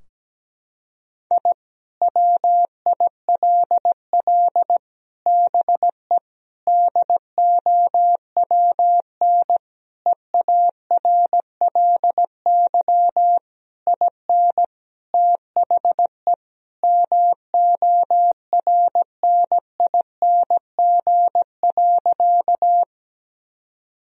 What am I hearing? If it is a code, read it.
Morse code, heard as a keyed tone, about 17 wpm, 700 Hz. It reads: I WILL BE DOWN EARLY IN THE MORNING.
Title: Sentences from Top 1000 Words - Review Part 2 - 17wpm